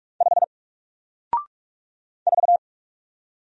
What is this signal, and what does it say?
Morse code, heard as a keyed tone, about 44 wpm, 700 Hz. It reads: 5 4
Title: Single Letter-Number Alt 45wpm